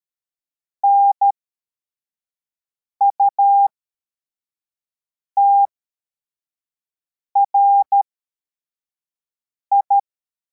Morse code, heard as a keyed tone, about 13 wpm, spaced as for 4 wpm, 800 Hz. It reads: NUTRI
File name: formation / cw / Text42